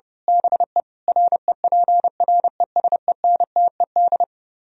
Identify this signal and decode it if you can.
Morse code, heard as a keyed tone, about 24 wpm, 700 Hz. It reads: BE REPRESENTED
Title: Speed-Racing - Sets of 5 Words 25wpm